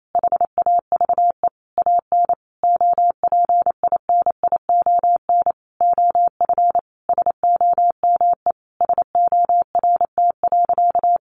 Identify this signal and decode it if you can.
Morse code, heard as a keyed tone, about 28 wpm, 700 Hz. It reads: HAVE AN OPINION OF SOME SORT.
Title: Sentences from Top 1300 Words 28wpm